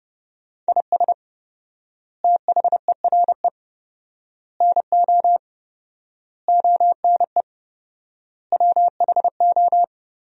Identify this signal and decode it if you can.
Morse code, heard as a keyed tone, about 30 wpm, 700 Hz. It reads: IS THERE NO ONE WHO
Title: 4x Word-Spacing - Sentences from Top 200 Words - 30wpm